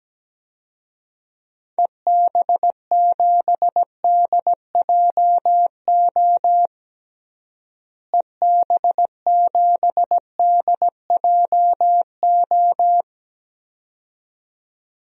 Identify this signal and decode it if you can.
Morse code, heard as a keyed tone, about 17 wpm, 700 Hz. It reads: EB7DJO EB7DJO
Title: DXCC-100 Call Signs - 6 Characters - Repeated 2x - 17wpm